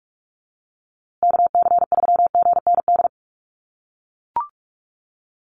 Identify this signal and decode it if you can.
Morse code, heard as a keyed tone, about 44 wpm, 700 Hz. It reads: KC3GND
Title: USA Call Signs Repeated 2x 45wpm